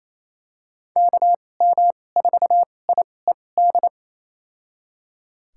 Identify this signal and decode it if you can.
Morse code, heard as a keyed tone, about 28 wpm, spaced as for 20 wpm, 700 Hz. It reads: KM4IED
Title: USA Call Signs - 2x Character Spacing - 28wpm